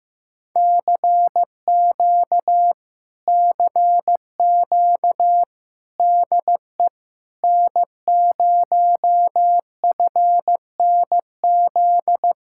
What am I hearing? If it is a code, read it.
Morse code, heard as a keyed tone, about 15 wpm, 700 Hz. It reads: CQ CQ DE N0FNZ
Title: Speed-Racing Common QSO Phrases Encore - 15wpm